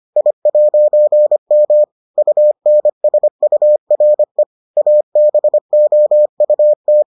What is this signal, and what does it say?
Morse code, heard as a keyed tone, about 25 wpm, 600 Hz. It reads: I'M UNSURE ABOUT